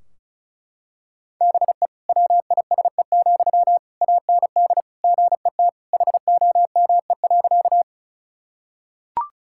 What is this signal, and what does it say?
Morse code, heard as a keyed tone, about 36 wpm, 700 Hz. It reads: BE WISE, AND GET HOME.
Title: Sentences from Top 1600 Words 35wpm